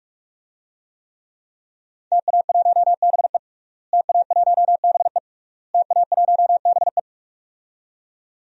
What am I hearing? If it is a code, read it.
Morse code, heard as a keyed tone, about 46 wpm, 700 Hz. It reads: TA1BE TA1BE TA1BE
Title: DXCC-100 Call Signs - 5 Characters - Repeated 3x - 45wpm